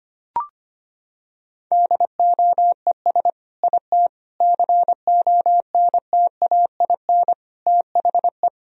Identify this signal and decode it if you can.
Morse code, heard as a keyed tone, about 25 wpm, 700 Hz. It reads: DOES IT CONTAIN THE
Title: Sentences from Top 1000 Words - Review Part 2 - 25wpm